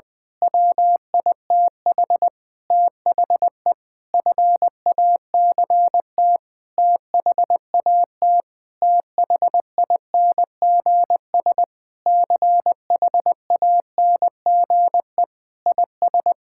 Morse code, heard as a keyed tone, about 21 wpm, 700 Hz. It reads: WITH THE FACT THAT THINGS CHANGE IS
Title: Sentences with 8-Word Phrases - 20wpm